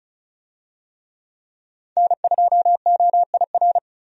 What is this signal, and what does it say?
Morse code, heard as a keyed tone, about 36 wpm, 700 Hz. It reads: N2OIR
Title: USA Call Signs - 5 Characters - 35wpm